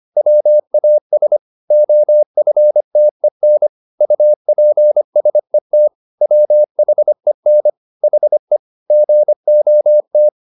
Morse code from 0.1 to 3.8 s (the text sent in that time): WAS OFTEN